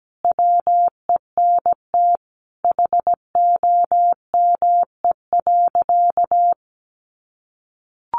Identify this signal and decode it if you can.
Morse code, heard as a keyed tone, about 17 wpm, 700 Hz. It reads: WENT HOME.
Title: Sentences from Top 200 Words 17wpm